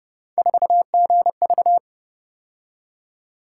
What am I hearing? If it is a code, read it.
Morse code, heard as a keyed tone, about 31 wpm, 700 Hz. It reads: 4GV